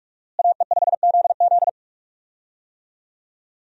Morse code, heard as a keyed tone, about 45 wpm, 700 Hz. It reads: AE5ZZ
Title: USA Call Signs - 5 Characters - Repeated 2x - 45wpm